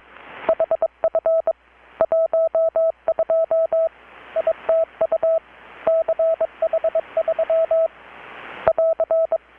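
Morse code, heard as a keyed tone, about 22 wpm, 650 Hz. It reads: HF 12 ITU CH3 <AR>
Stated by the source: Morse from KPH's Night of Nights XX maritime radiotelegraphy event, received in Chiapas, Mexico about 3,800 km away on a Kenwood R-600 with a grounded 10 m long-wire antenna.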